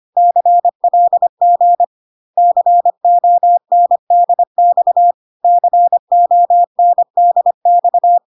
Morse code, heard as a keyed tone, about 25 wpm, 700 Hz. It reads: CLG CONDX CONDX
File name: CW abbreviations alphabetical repeated 3 times 25 25wpm